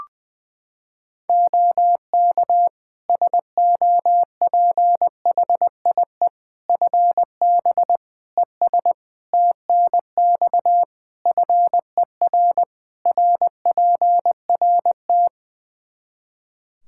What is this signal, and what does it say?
Morse code, heard as a keyed tone, about 20 wpm, 700 Hz. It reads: OK SOPHIE FB ES TNX FER RPRT
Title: Practice QSO - Exchange 2 - Info - 20wpm